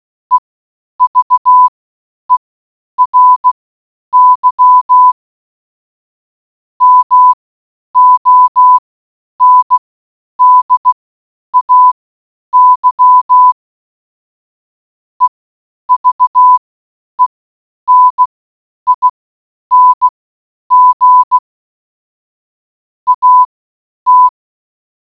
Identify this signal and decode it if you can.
Morse code, heard as a keyed tone, about 16 wpm, spaced as for 9 wpm, 1000 Hz. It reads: EVERY MONDAY EVENING AT